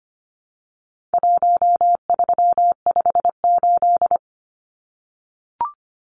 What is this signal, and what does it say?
Morse code, heard as a keyed tone, about 25 wpm, 700 Hz. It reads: 1358
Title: Numbers - Contest Style - 4-Digit 25wpm